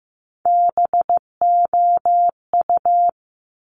Morse code, heard as a keyed tone, about 15 wpm, 700 Hz. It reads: BOU
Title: Top Three Letter Combinations in Words 15wpm